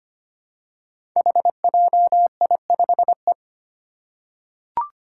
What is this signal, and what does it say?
Morse code, heard as a keyed tone, about 25 wpm, 700 Hz. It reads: HJI5E
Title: Five Challenging Letter-Number Combinations 25wpm